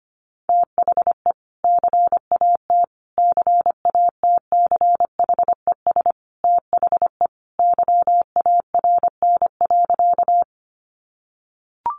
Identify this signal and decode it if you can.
Morse code, heard as a keyed tone, about 25 wpm, 700 Hz. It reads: THE CAT CATCHES THE YARN.
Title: Additional Easy Phrases 25wpm